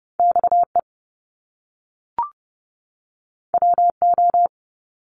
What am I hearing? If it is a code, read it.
Morse code, heard as a keyed tone, about 30 wpm, 700 Hz. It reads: XE WO
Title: Top Two Letter Combinations in Words 30wpm